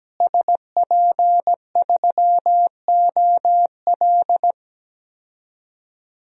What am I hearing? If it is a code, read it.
Morse code, heard as a keyed tone, about 17 wpm, 700 Hz. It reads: SP3OL
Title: DXCC-100 Call Signs - 5 Characters - Repeated 2x - 17wpm